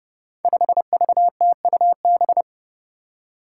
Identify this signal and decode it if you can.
Morse code, heard as a keyed tone, about 30 wpm, 700 Hz. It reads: 5VTUB
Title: Five Challenging Letter-Number Combinations 30wpm